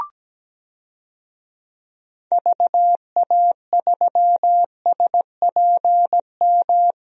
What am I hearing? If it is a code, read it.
Morse code, heard as a keyed tone, about 17 wpm, 700 Hz. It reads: VA3SPM
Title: DXCC-100 Call Signs - 6 Characters - Repeated 3x - 17wpm